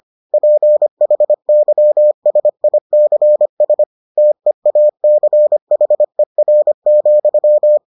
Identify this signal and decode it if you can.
Morse code, heard as a keyed tone, about 25 wpm, 600 Hz. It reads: PHYSICS TEACHER,